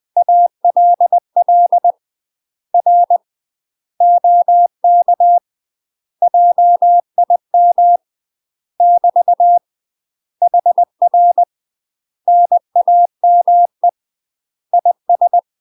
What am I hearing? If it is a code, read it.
Morse code, heard as a keyed tone, about 20 wpm, 700 Hz. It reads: ALL R OK JIM <BT> HR NAME IS
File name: UK MASTER QSOs TRAINING / 20WPM / UK MASTER QSOs 0043